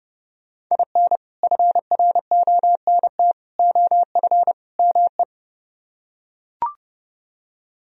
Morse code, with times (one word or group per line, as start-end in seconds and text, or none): 0.7-1.2 IN
1.4-3.3 FRONT
3.6-4.5 OF
4.8-5.2 ME